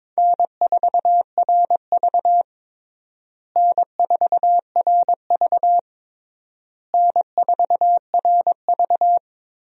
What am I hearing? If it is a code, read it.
Morse code, heard as a keyed tone, about 22 wpm, 700 Hz. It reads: N4RV N4RV N4RV
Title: USA Call Signs - 4 Characters - Repeated 3x - 22wpm